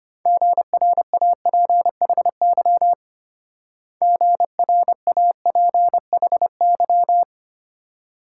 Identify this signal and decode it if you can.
Morse code, heard as a keyed tone, about 32 wpm, 700 Hz. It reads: GRAPHY GRAPHY